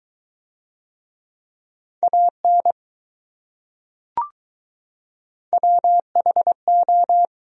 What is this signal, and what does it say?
Morse code, heard as a keyed tone, about 20 wpm, 700 Hz. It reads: AN WHO